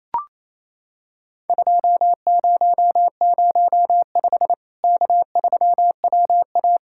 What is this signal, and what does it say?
Morse code, heard as a keyed tone, about 28 wpm, 700 Hz. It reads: 2005 K3WA
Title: Worked All Europe DX Contest - QTC Exchange - 28wpm